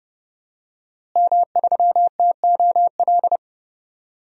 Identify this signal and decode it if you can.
Morse code, heard as a keyed tone, about 30 wpm, 700 Hz. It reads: M3TOL
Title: DXCC-100 Call Signs - 5 Characters - Repeated 3x - 30wpm